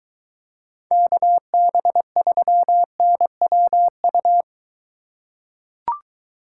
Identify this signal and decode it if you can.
Morse code, heard as a keyed tone, about 23 wpm, 700 Hz. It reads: KB3NWU